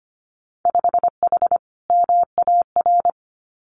Morse code, heard as a keyed tone, about 25 wpm, 700 Hz. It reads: 5H MAR
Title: Winter Field Day Exchanges Encore 25wpm